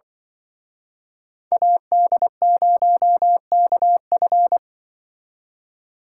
Morse code, heard as a keyed tone, about 24 wpm, 700 Hz. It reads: AD0KF
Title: Speed Racing - USA 2x2 Call Signs 20wpm